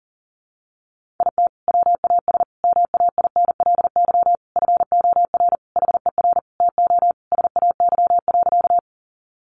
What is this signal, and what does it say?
Morse code, heard as a keyed tone, about 40 wpm, 700 Hz. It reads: IT WAS MAINLY FOR HER TO SAY.
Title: Sentences from Top 1400 Words 40wpm